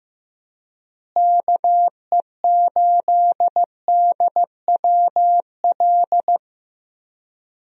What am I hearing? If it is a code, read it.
Morse code, heard as a keyed tone, about 16 wpm, 700 Hz. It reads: KE8DWL
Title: USA Call Signs - 6 Characters - Repeated 3x - 15wpm